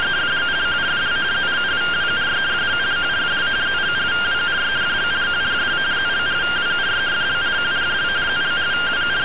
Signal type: two-tone teleprinter signal

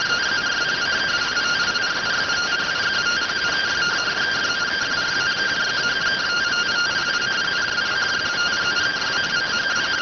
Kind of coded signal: two-tone teleprinter signal

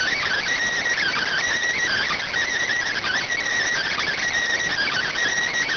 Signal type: Emergency Alert System (SAME) tones

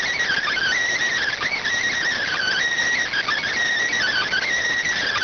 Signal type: Emergency Alert System (SAME) tones

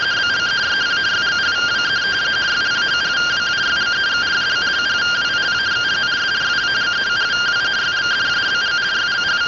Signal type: two-tone teleprinter signal

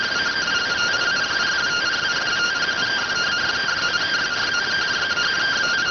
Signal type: two-tone teleprinter signal

